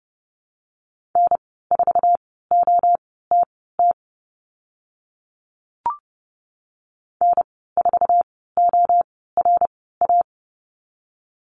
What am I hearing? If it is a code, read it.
Morse code, heard as a keyed tone, about 32 wpm, spaced as for 17 wpm, 700 Hz. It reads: N4OTT N4ORA
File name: Lesson 14g - Number 2 - Call Signs 30wpm